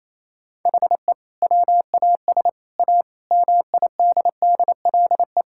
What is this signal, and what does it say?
Morse code, heard as a keyed tone, about 28 wpm, 700 Hz. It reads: HE WAS A MIDDLE
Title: Sentences from Top 1000 Words - Review Part 6 - 28wpm